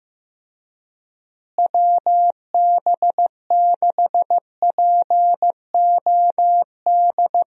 Morse code, heard as a keyed tone, about 16 wpm, 700 Hz. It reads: WB6POD
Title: USA Call Signs - 6 Characters - Repeated 3x - 15wpm